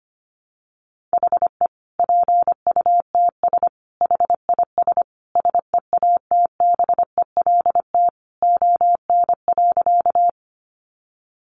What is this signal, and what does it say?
Morse code, heard as a keyed tone, about 25 wpm, 700 Hz. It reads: HE PUTS HIS SEATBELT ON.
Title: Additional Easy Phrases 25wpm